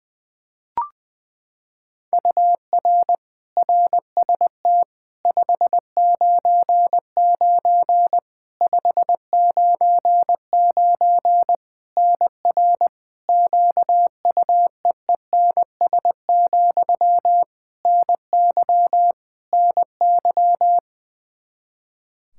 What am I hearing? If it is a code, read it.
Morse code, heard as a keyed tone, about 20 wpm, 700 Hz. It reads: UR RST 599 599 NR QUEENS, NY NY
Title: K1USN SST QSO Practice 20wpm